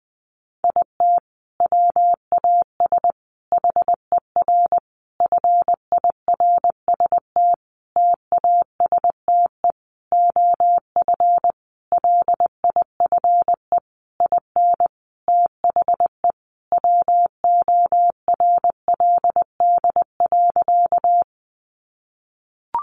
Morse code, heard as a keyed tone, about 20 wpm, 700 Hz. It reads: IT WAS HER FIRST TASTE OF LIFE IN THE WORLD.
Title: Sentences from Top 1100 Words 20wpm